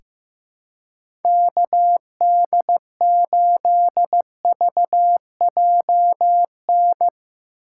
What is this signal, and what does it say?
Morse code, heard as a keyed tone, about 15 wpm, 700 Hz. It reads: KD8VJN
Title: USA Call Signs - 6 Characters - 3x Character Spacing - 15wpm